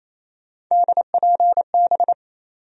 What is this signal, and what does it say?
Morse code, heard as a keyed tone, about 28 wpm, 700 Hz. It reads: DPB
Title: USA Call Sign Postfixes - 3 Letters 28wpm